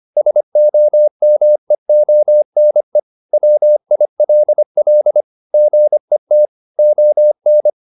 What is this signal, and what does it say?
Morse code, heard as a keyed tone, about 25 wpm, 600 Hz. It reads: SOMEONE WILL GET ON